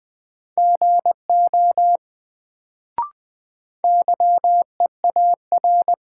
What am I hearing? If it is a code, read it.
Morse code, heard as a keyed tone, about 20 wpm, 700 Hz. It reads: GO YEAR
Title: Sets of 2 Words - New Word Drill - 20wpm